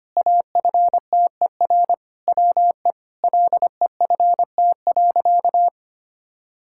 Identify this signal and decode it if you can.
Morse code, heard as a keyed tone, about 25 wpm, 700 Hz. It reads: AFTER WE LEFT.
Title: Sentences from Top 1000 Words - Review Part 6 - 25wpm